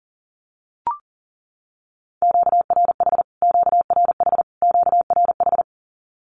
Morse code, heard as a keyed tone, about 40 wpm, 700 Hz. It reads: QRH QRH QRH
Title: Q-Codes Repeated 3x 40wpm